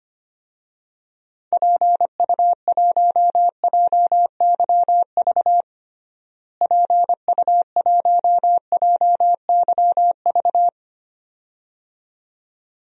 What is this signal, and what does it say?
Morse code, heard as a keyed tone, about 25 wpm, 700 Hz. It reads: PU1JYV PU1JYV
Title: DXCC-100 Call Signs - 6 Characters - Repeated 2x - 25wpm